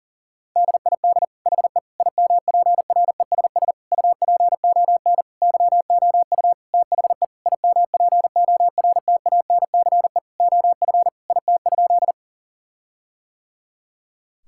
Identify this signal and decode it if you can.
Morse code, heard as a keyed tone, about 37 wpm, 700 Hz. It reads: DID HE IMPRESS UPON YOU THE IMPORTANCE OF IT?
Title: Sentences from Top 1600 Words 40wpm